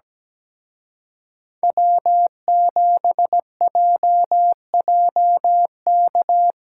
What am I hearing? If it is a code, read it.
Morse code, heard as a keyed tone, about 19 wpm, 700 Hz. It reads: W7JJK
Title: Speed Racing - USA 1x3 Call Signs 17wpm